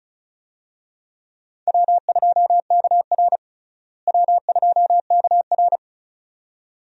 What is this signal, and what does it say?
Morse code, heard as a keyed tone, about 36 wpm, 700 Hz. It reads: W2KR W2KR
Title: USA Call Signs - 4 Characters - Repeated 2x - 35wpm